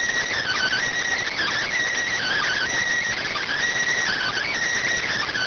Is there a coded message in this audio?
Yes: Emergency Alert System (SAME) tones from 0.4 to 4.5 s, heard as harsh buzzing bursts.